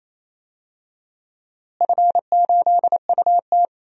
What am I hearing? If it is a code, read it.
Morse code, heard as a keyed tone, about 28 wpm, 700 Hz. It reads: F8UT
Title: DXCC-100 Call Signs - 4 Characters - Repeated 3x - 28wpm